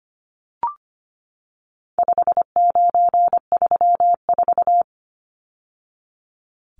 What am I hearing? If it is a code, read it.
Morse code, heard as a keyed tone, about 25 wpm, 700 Hz. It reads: 5934
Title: Numbers - 4-Digit 25wpm